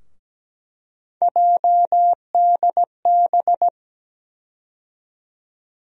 Morse code, heard as a keyed tone, about 17 wpm, 700 Hz. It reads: JDB
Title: USA Call Sign Postfixes - 3 Letters 17wpm